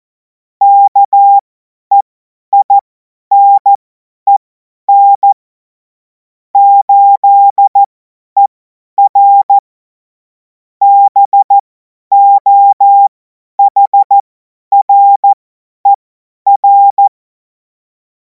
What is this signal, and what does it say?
Morse code, heard as a keyed tone, about 14 wpm, spaced as for 10 wpm, 800 Hz. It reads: KEINEN 8ER BOHRER
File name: Morsekurs / Witze / Witz 31. März 2026 50 F70